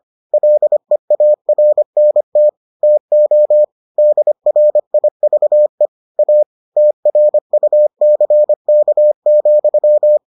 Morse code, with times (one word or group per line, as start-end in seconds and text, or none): 0.3-2.5 LEARNT
2.8-3.6 TO
4.0-5.9 DRIVE
6.2-6.4 A
6.8-10.2 TRUCK,